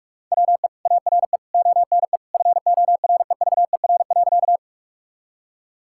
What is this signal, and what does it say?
Morse code, heard as a keyed tone, about 46 wpm, 700 Hz. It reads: WE ARE ONE FOREVER.
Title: Sentences from Top 1600 Words 45wpm